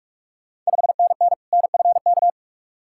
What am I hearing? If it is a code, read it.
Morse code, heard as a keyed tone, about 46 wpm, 700 Hz. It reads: 5NN NFK
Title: Ontario QSO Party Exchange 45wpm